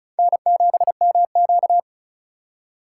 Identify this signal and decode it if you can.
Morse code, heard as a keyed tone, about 36 wpm, 700 Hz. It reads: N7MQ
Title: USA Call Signs - 4 Characters - 8x Character Spacing - 35wpm